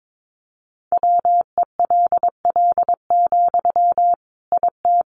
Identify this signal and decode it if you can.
Morse code, heard as a keyed tone, about 22 wpm, 700 Hz. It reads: WELL, IT
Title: Sentences from Top 1500 Words 22wpm